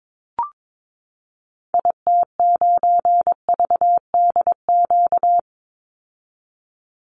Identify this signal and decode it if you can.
Morse code, heard as a keyed tone, about 22 wpm, 700 Hz. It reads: IT9VDQ
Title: Call Signs - 22wpm